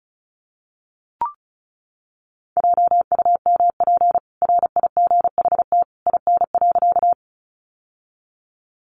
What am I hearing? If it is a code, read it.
Morse code, heard as a keyed tone, about 34 wpm, 700 Hz. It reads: JUMP RIGHT IN.
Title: Sentences from Top 1300 Words 35wpm